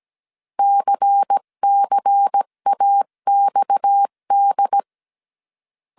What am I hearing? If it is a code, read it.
Morse code, heard as a keyed tone, about 17 wpm, 800 Hz. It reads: CCAXD
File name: CW / DET / 1 6 1500C S C 16wpm 2733